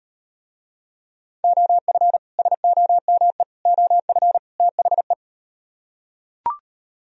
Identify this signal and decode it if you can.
Morse code, heard as a keyed tone, about 38 wpm, 700 Hz. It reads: OF SOME OF THE